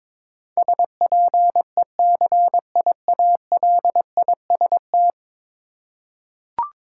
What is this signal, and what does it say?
Morse code, heard as a keyed tone, about 22 wpm, 700 Hz. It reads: SPECIALIST
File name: practice / 22wpm / 10 Letter Words - First 3 Letters Given - 22wpm